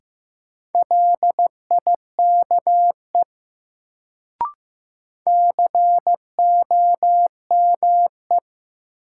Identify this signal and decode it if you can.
Morse code, heard as a keyed tone, about 15 wpm, 700 Hz. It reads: LIKE COME
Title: Sentences from Top 100 Words - New Word Drill - 15wpm